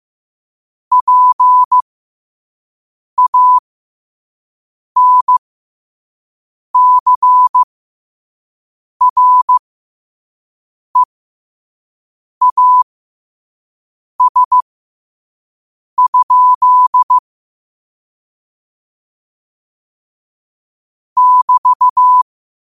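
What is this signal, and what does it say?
Morse code, heard as a keyed tone, about 15 wpm, spaced as for 5 wpm, 1000 Hz. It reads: PANCREAS? <BT>
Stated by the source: Morse code sent at 5 wpm.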